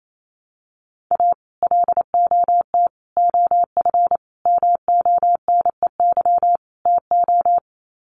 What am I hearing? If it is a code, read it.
Morse code, heard as a keyed tone, about 28 wpm, 700 Hz. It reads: A LOT OF MONEY TO
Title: Sets of 5 Words Encore2 28wpm